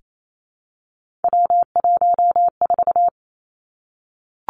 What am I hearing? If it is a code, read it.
Morse code, heard as a keyed tone, about 28 wpm, 700 Hz. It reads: W14
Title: Three Challenging Letter-Number Combinations 28wpm